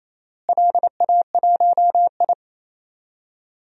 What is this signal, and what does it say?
Morse code, heard as a keyed tone, about 28 wpm, 700 Hz. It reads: LA1I
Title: DXCC-100 Call Signs - 4 Characters - Repeated 2x - 28wpm